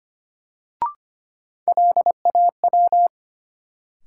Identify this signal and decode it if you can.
Morse code, heard as a keyed tone, about 25 wpm, 700 Hz. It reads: LAW
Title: Sentences from Top 600 Words - New Word Drill - 25wpm